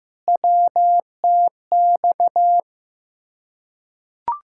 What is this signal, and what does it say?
Morse code, heard as a keyed tone, about 15 wpm, 700 Hz. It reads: WTX
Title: Field Day Exchanges Encore - 15wpm